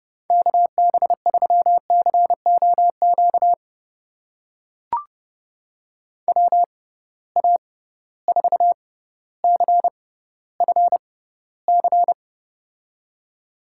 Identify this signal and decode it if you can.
Morse code, heard as a keyed tone, about 29 wpm, spaced as for 15 wpm, 700 Hz. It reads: KB3COQ WA4CFC
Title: USA Call Signs - 6 Characters - 6x Character Spacing - 30wpm